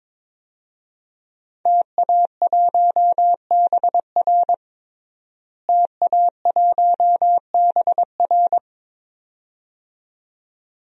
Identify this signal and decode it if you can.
Morse code, heard as a keyed tone, about 22 wpm, 700 Hz. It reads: TA1BR TA1BR
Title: DXCC-100 Call Signs - 5 Characters - Repeated 2x - 22wpm